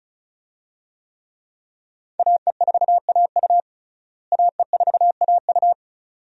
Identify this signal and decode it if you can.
Morse code, heard as a keyed tone, about 34 wpm, 700 Hz. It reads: AE4AU AE4AU
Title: USA Call Signs - 5 Characters - Repeated 2x - 35wpm